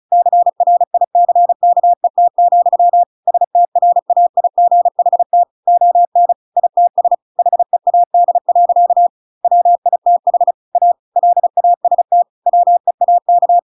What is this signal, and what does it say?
Morse code, heard as a keyed tone, about 34 wpm, 700 Hz. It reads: CRICKET, STRAIGHT ON ITS HEAD. WITH A LAST WEAK